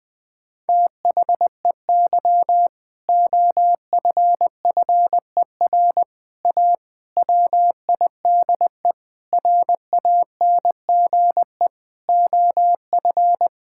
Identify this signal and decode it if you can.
Morse code, heard as a keyed tone, about 20 wpm, 700 Hz. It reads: THEY OFFER A WIDE RANGE OF
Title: Sentences with 4-Word Phrases - 20wpm